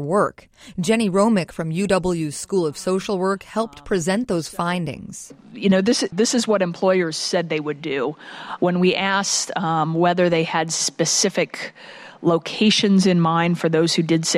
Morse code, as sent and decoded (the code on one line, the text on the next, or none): none